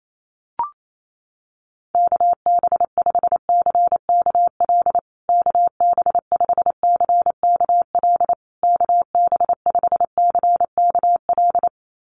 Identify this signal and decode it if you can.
Morse code, heard as a keyed tone, about 28 wpm, 700 Hz. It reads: KB5CKL KB5CKL KB5CKL
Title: USA Call Signs Repeated 3x 28wpm